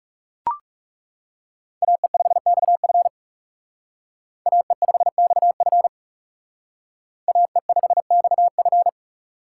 Morse code, heard as a keyed tone, about 38 wpm, 700 Hz. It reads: AE5XF AE5XF AE5XF